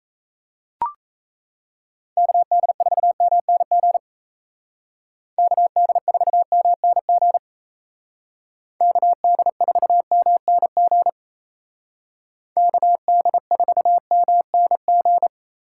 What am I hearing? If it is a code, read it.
Morse code, heard as a keyed tone, about 35 wpm, 700 Hz. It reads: KD4MNG KD4MNG KD4MNG KD4MNG
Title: Speed Racing - USA 2x3 Call Signs 28wpm